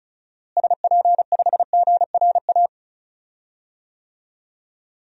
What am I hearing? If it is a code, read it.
Morse code, heard as a keyed tone, about 36 wpm, 700 Hz. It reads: SP5GRA
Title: DXCC-100 Call Signs - 6 Characters - Repeated 2x - 35wpm